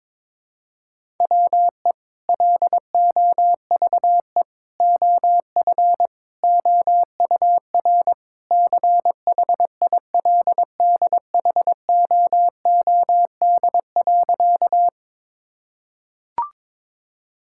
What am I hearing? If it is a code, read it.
Morse code, heard as a keyed tone, about 22 wpm, 700 Hz. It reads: WE LOVE OF OUR CHILDHOOD.
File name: practice / 22wpm / Sentences with 2-Word Phrases - 22wpm